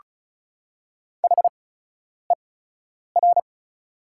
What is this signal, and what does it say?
Morse code, heard as a keyed tone, about 36 wpm, spaced as for 10 wpm, 700 Hz. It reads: HER
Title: 8x Character Spacing - Top 100 Words - 35wpm